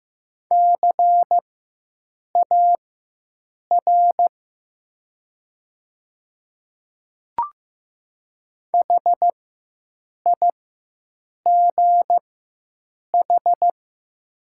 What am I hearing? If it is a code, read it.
Morse code, heard as a keyed tone, about 15 wpm, spaced as for 7 wpm, 700 Hz. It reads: CAR HIGH
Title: 4x Character Spacing - Top 300 Words - 15wpm